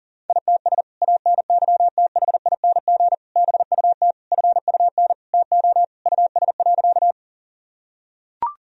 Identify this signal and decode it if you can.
Morse code, heard as a keyed tone, about 40 wpm, 700 Hz. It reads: ITS ANYTHING BUT FUN TO US.
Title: Sentences from Top 1000 Words - Review Part 4 - 40wpm